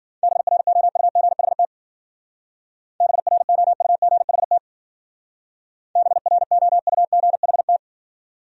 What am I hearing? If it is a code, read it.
Morse code, heard as a keyed tone, about 50 wpm, 700 Hz. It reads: BROUGHT BROUGHT BROUGHT